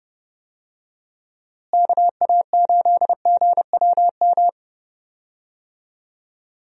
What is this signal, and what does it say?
Morse code, heard as a keyed tone, about 30 wpm, 700 Hz. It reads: KA8GWM
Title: USA Call Signs - 6 Characters - Repeated 2x - 30wpm